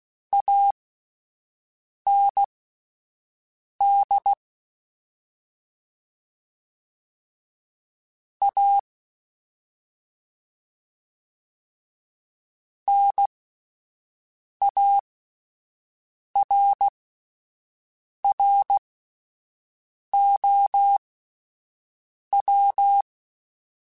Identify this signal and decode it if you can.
Morse code, heard as a keyed tone, about 16 wpm, spaced as for 5 wpm, 800 Hz. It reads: AND A NARROW
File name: Spaced / Chuckles Spaced-mp3s / MiddleAge sp @10